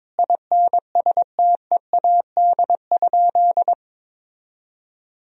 Morse code, heard as a keyed tone, about 22 wpm, 700 Hz. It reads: INSTEAD?
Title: Sentences from Top 1000 Words - Review Part 4 - 22wpm